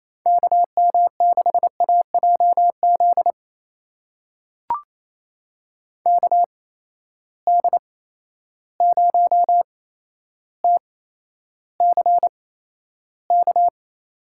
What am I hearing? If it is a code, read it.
Morse code, heard as a keyed tone, about 28 wpm, spaced as for 12 wpm, 700 Hz. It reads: KM6AJZ KD0TCK
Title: USA Call Signs - 6 Characters - 8x Character Spacing - 28wpm